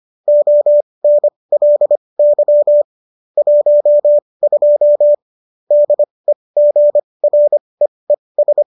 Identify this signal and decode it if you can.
Morse code, heard as a keyed tone, about 25 wpm, spaced as for 20 wpm, 590 Hz. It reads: ONLY 12 DEGREES
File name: qso223 20